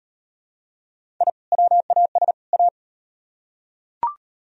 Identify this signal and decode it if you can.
Morse code, heard as a keyed tone, about 34 wpm, 700 Hz. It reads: I WAS A